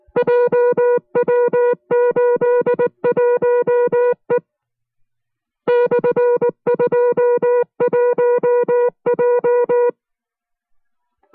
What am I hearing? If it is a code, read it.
Morse code, heard as a keyed tone, about 19 wpm, 490 Hz. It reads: JW81E /21J